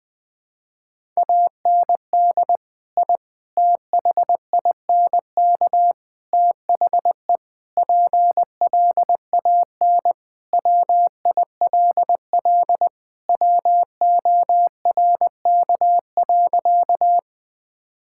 Morse code, heard as a keyed tone, about 21 wpm, 700 Hz. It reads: AND I THINK THE PLAN WILL WORK.
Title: Sentences with 4-Word Phrases - 20wpm